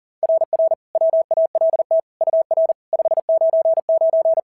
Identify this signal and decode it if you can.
Morse code, heard as a keyed tone, about 40 wpm, 690 Hz. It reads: RR WALT UR 599
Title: SOTA QSO Practice 40wpm